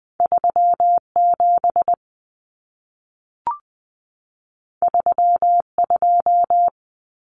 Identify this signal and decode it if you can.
Morse code, heard as a keyed tone, about 20 wpm, 700 Hz. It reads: 37 32
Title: Common Cut Numbers - 2-Digit 20wpm